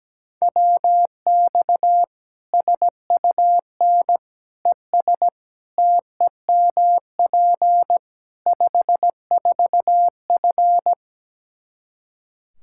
Morse code, heard as a keyed tone, about 17 wpm, 700 Hz. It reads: WX SUN ES TEMP 54F
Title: Practice QSO - Exchange 2 - Info - 17wpm